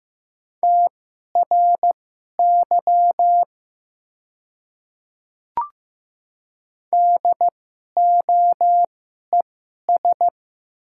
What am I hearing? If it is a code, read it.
Morse code, heard as a keyed tone, about 15 wpm, spaced as for 11 wpm, 700 Hz. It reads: TRY DOES
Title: 2x Character Spacing - Top 200 Words - 15wpm